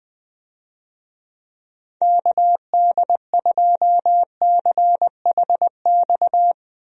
Morse code, heard as a keyed tone, about 20 wpm, 700 Hz. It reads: KD2CHX
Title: USA Call Signs - 6 Characters - Repeated 2x - 20wpm